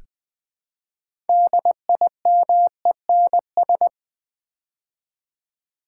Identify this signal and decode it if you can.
Morse code, heard as a keyed tone, about 20 wpm, 700 Hz. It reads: DIMENS
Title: Build A Word - 3 Letters At A Time - 9-Letter Words - 20wpm